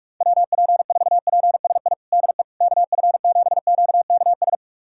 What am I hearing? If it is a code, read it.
Morse code, heard as a keyed tone, about 42 wpm, 700 Hz. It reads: WP4PSI DE KF7QKS
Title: USA Callsign DE Callsign - 6 Characters - 45wpm